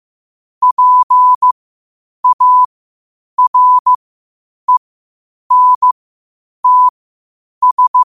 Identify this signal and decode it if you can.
Morse code, heard as a keyed tone, about 15 wpm, spaced as for 8 wpm, 1000 Hz. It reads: PARENTS
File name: qotdfiles / 8wpm